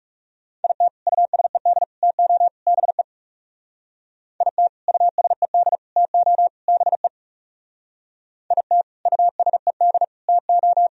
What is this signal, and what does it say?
Morse code, heard as a keyed tone, about 40 wpm, 700 Hz. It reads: IT USED TO BE IT USED TO BE IT USED TO